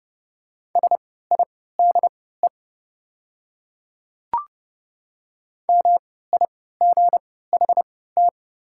Morse code, heard as a keyed tone, about 29 wpm, spaced as for 17 wpm, 690 Hz. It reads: SIDE MIGHT
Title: 3x Character Spacing - Top 300 Words - 30wpm